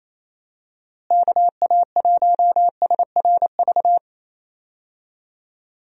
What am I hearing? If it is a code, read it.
Morse code, heard as a keyed tone, about 28 wpm, 700 Hz. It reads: KA1SRV
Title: USA Call Signs - 6 Characters - Repeated 3x - 28wpm